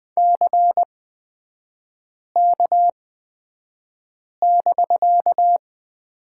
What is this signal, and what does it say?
Morse code, heard as a keyed tone, about 20 wpm, spaced as for 5 wpm, 690 Hz. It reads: CK<BK>
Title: 255 Lesson 41e - Period - Three Characters 20wpm